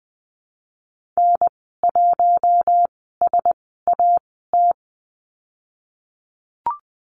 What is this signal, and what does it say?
Morse code, heard as a keyed tone, about 20 wpm, spaced as for 15 wpm, 700 Hz. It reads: N1SAT
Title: 048 Lesson 9g - Number 4 - Call Signs 20wpm